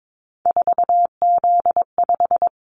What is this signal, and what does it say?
Morse code, heard as a keyed tone, about 22 wpm, 700 Hz. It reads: 4Z5
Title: Callsign Prefixes for Top 50 DXCC - 22wpm